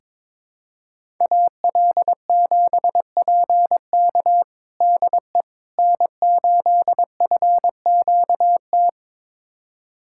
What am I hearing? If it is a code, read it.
Morse code, heard as a keyed tone, about 22 wpm, 700 Hz. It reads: AL7PK DE N8FQT
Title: USA Callsign DE Callsign - 5 Characters - 22wpm